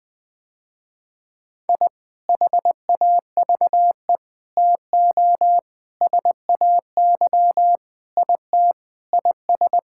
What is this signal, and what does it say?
Morse code, heard as a keyed tone, about 20 wpm, 700 Hz. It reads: I HAVE TO SAY IT IS
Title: Sentences with 4-Word Phrases - Spoken Beginning - 20wpm